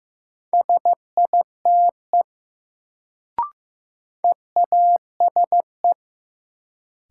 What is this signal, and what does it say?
Morse code, heard as a keyed tone, about 15 wpm, 700 Hz. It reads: SITE EASE
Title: Sentences from Top 1000 Words - Review Part 1 - New Word Drill - 15wpm